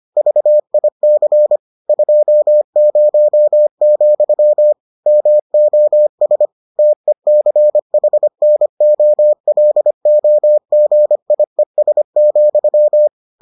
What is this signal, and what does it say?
Morse code, heard as a keyed tone, about 25 wpm, 600 Hz. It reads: VIC 20, MOS TECHNOLOGIES,